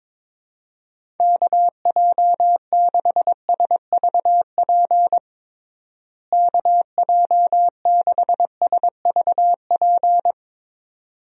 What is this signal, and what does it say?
Morse code, heard as a keyed tone, about 22 wpm, 700 Hz. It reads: KJ6SVP KJ6SVP
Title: USA Call Signs - 6 Characters - Repeated 2x - 22wpm